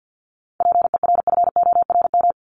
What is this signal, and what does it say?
Morse code, heard as a keyed tone, about 50 wpm, 700 Hz. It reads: PERFORM
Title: ED Suffix - Top 100 Words with Base Words 50wpm